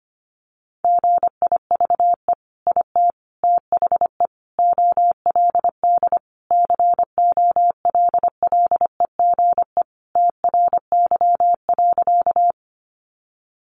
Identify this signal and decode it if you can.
Morse code, heard as a keyed tone, about 25 wpm, 700 Hz. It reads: GIVE IT THE OLD COLLEGE TRY.
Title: Sentences from Top 1300 Words 25wpm